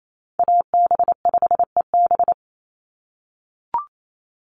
Morse code, heard as a keyed tone, about 28 wpm, 700 Hz. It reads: AB5EB
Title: Call Signs - 28wpm